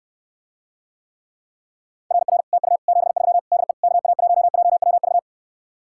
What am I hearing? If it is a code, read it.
Morse code, heard as a keyed tone, about 69 wpm, 700 Hz. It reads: RR TU 73 DE KA1JWV